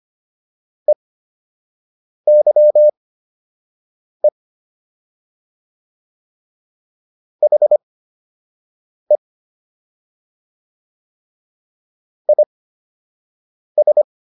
Morse code, heard as a keyed tone, about 25 wpm, spaced as for 6 wpm, 600 Hz. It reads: EYE HE IS